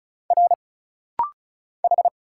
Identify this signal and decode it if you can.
Morse code, heard as a keyed tone, about 36 wpm, 700 Hz. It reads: R H
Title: Single Letters - Rapid-Fire 35wpm